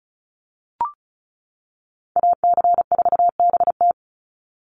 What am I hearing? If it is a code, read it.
Morse code, heard as a keyed tone, about 34 wpm, 700 Hz. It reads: AC4BT
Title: Call Signs - 35wpm